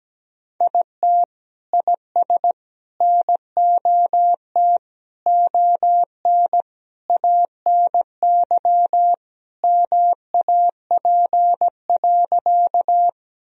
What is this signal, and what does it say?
Morse code, heard as a keyed tone, about 17 wpm, 700 Hz. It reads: IT IS NOT ON ANY MAP.
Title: Sentences from Top 1000 Words - Review Part 2 - 17wpm